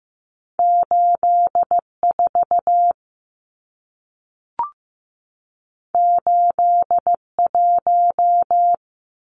Morse code, heard as a keyed tone, about 15 wpm, 700 Hz. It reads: 84 81
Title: Numbers - 2-Digit 15wpm